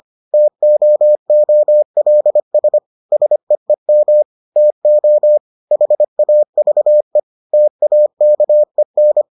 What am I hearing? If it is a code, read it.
Morse code, heard as a keyed tone, about 25 wpm, 600 Hz. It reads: TOOLS SEEM TO HAVE TAKEN